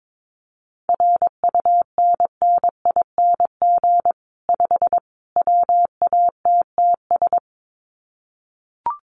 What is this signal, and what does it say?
Morse code, heard as a keyed tone, about 22 wpm, 700 Hz. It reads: RUNNING 5 WATTS
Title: Common QSO Phrases 22wpm